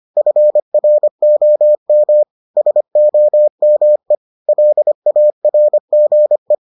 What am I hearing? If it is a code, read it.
Morse code, heard as a keyed tone, about 25 wpm, 600 Hz. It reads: FROM SOME LARGE